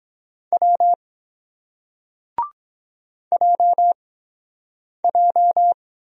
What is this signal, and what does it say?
Morse code, heard as a keyed tone, about 23 wpm, spaced as for 7 wpm, 700 Hz. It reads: W JJ